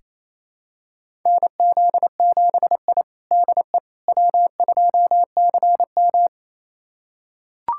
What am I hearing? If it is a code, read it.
Morse code, heard as a keyed tone, about 27 wpm, 700 Hz. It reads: NZ7I DE W2CM
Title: USA Callsign DE Callsign - 4 Characters - 28wpm